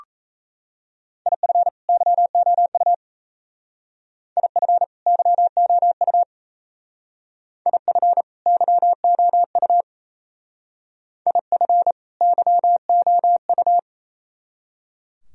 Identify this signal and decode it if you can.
Morse code, heard as a keyed tone, about 35 wpm, 690 Hz. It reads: IF YOU IF YOU IF YOU IF YOU
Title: Speed-Racing - Sets of 2 Words 28wpm